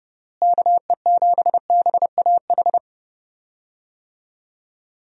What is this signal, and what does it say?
Morse code, heard as a keyed tone, about 31 wpm, 700 Hz. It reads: KE7BAH
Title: USA Call Signs - 6 Characters - Repeated 2x - 30wpm